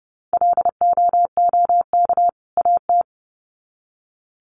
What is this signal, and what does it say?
Morse code, heard as a keyed tone, about 30 wpm, 700 Hz. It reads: LOOK AT
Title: Sets of 2 Words 30wpm